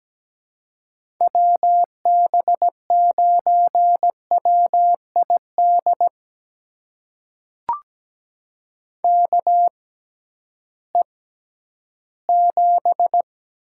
Morse code, heard as a keyed tone, about 17 wpm, spaced as for 11 wpm, 700 Hz. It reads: WB9WID KE7
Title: USA Call Signs - 6 Characters - 6x Character Spacing - 17wpm